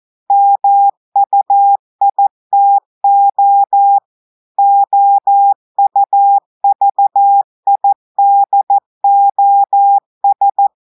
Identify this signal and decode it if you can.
Morse code, heard as a keyed tone, about 14 wpm, 800 Hz. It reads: MUITO OUVIDOS
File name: audios / portugais / portugais texte en clair 2 vitesse 14 mots minutes